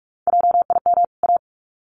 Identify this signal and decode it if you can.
Morse code, heard as a keyed tone, about 44 wpm, 700 Hz. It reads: JIM A
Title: CWT Exchange 45wpm